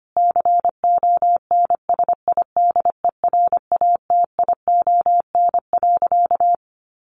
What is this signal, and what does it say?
Morse code, heard as a keyed tone, about 25 wpm, 700 Hz. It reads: CONSIDERATION.
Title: Sentences from Top 1500 Words 25wpm